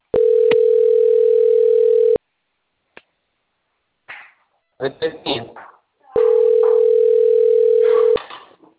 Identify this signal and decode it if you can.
Ringback tone (North America), 440 + 480 Hz, with speech.
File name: out- - - - - .60554